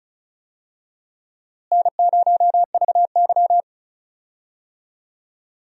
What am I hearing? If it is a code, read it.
Morse code, heard as a keyed tone, about 34 wpm, 700 Hz. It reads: N0VY
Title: USA Call Signs - 4 Characters - Repeated 2x - 35wpm